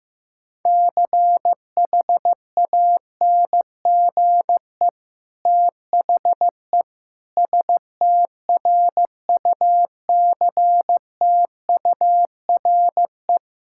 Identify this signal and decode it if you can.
Morse code, heard as a keyed tone, about 15 wpm, 700 Hz. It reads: CHANGE THE STRUCTURE
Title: Sentences from Top 1000 Words - Review Part 6 - 15wpm